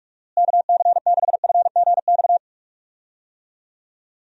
Morse code, heard as a keyed tone, about 44 wpm, 700 Hz. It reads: KC6FGX
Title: USA Call Signs - 6 Characters - Repeated 3x - 45wpm